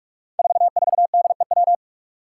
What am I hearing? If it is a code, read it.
Morse code, heard as a keyed tone, about 46 wpm, 700 Hz. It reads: 44DEW
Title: Five Challenging Letter-Number Combinations 45wpm